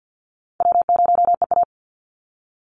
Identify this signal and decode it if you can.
Morse code, heard as a keyed tone, about 50 wpm, 700 Hz. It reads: W0EA
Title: Easy Call Signs - 50wpm